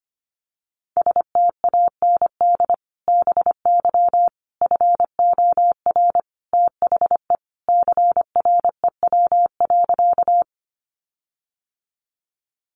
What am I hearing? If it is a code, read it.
Morse code, heard as a keyed tone, about 25 wpm, 700 Hz. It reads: STAND BY FOR THE CREW.
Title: Sentences from Top 1500 Words 25wpm